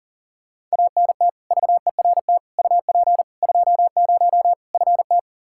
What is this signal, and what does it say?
Morse code, heard as a keyed tone, about 40 wpm, 700 Hz. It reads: ANT VERT UP 20 FT
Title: Practice QSO - Exchange 2 - Info - 40wpm